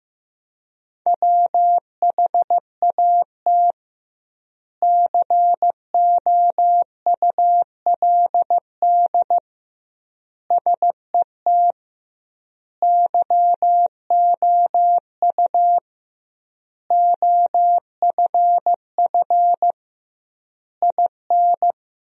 Morse code, heard as a keyed tone, about 15 wpm, 700 Hz. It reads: WHAT COULD SET YOU OFF IN
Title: 2x Word-Spacing - Sentences from Top 200 Words - 15wpm